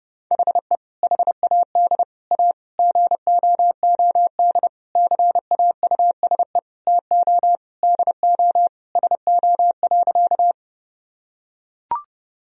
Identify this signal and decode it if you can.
Morse code, heard as a keyed tone, about 32 wpm, 700 Hz. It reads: HE HAD A GOOD CAUSE TO DO SO.
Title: Sentences from Top 1000 Words - Review Part 1 - 30wpm